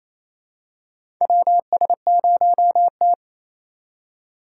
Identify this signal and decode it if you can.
Morse code, heard as a keyed tone, about 28 wpm, 700 Hz. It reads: WS0T
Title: USA Call Signs - 4 Characters - 3x Character Spacing - 28wpm